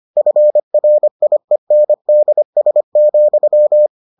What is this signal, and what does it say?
Morse code, heard as a keyed tone, about 25 wpm, 600 Hz. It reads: FRIENDS,